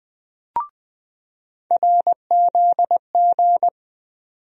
Morse code, heard as a keyed tone, about 20 wpm, 700 Hz. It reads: RZG
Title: USA Call Sign Postfixes - 3 Letters 20wpm